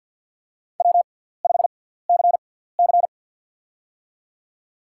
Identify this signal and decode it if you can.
Morse code, heard as a keyed tone, about 53 wpm, spaced as for 17 wpm, 700 Hz. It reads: W5CC
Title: USA Call Signs - 6x Character Spacing - 50wpm